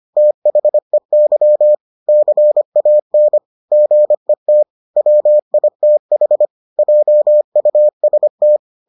Morse code, heard as a keyed tone, about 25 wpm, 600 Hz. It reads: THEY CAN GET WITH JUST